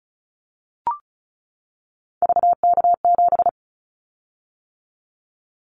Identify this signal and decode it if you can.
Morse code, heard as a keyed tone, about 34 wpm, 700 Hz. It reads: VK7
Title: Callsign Prefixes for Top 50 DXCC - 35wpm